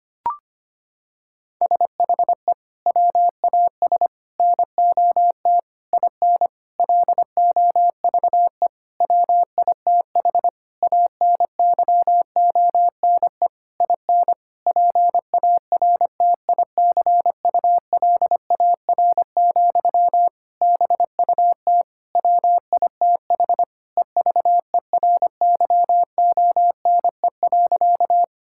Morse code, heard as a keyed tone, about 25 wpm, 700 Hz. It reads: SHE WAS NOT IN LOVE WITH ANYONE IN PARTICULAR, BUT WITH EVERYONE.
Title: Sentences from Top 1000 Words - Review Part 5 - 25wpm